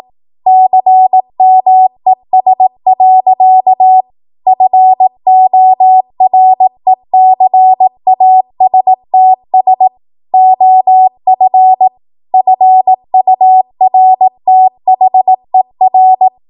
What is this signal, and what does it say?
Morse code, heard as a keyed tone, about 18 wpm, 750 Hz. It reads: CMES. FORECASTS OF FURTHER
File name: Morse / 18WPM